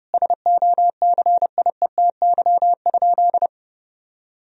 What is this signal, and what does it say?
Morse code, heard as a keyed tone, about 30 wpm, 700 Hz. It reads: SOCIETY?
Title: Sentences from Top 1000 Words - Review Part 4 - 30wpm